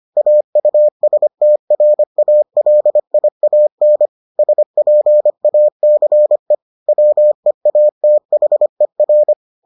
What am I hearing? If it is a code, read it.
Morse code, heard as a keyed tone, about 25 wpm, 600 Hz. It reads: AUSTRALIAN SPACE WEATHER